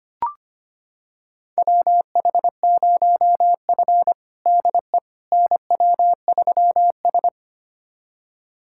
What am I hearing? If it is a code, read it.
Morse code, heard as a keyed tone, about 25 wpm, 700 Hz. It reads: WH0F DE NW3S
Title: USA Callsign DE Callsign - 4 Characters - 25wpm